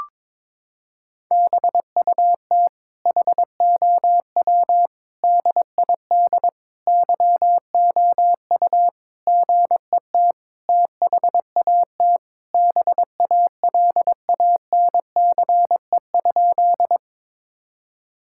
Morse code, heard as a keyed tone, about 21 wpm, 700 Hz. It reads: BUT HOW DID YOU GET THAT BALANCE?
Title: Sentences from Top 1000 Words - Review Part 6 - 22wpm